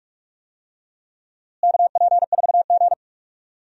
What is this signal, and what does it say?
Morse code, heard as a keyed tone, about 46 wpm, 700 Hz. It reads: KP4G
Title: DXCC-100 Call Signs - 4 Characters - Repeated 2x - 45wpm